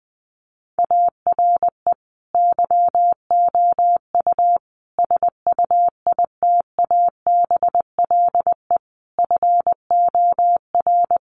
Morse code, heard as a keyed tone, about 20 wpm, 700 Hz. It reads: ARE YOU SUITABLE FOR
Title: Sentences from Top 1400 Words 20wpm